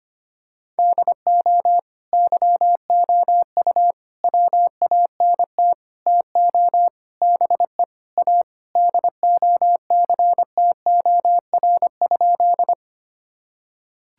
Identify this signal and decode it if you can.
Morse code, heard as a keyed tone, about 25 wpm, 700 Hz. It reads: DO YOU WANT TO BE A DOCTOR?
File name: practice / 25wpm / Sentences with 4-Word Phrases - 25wpm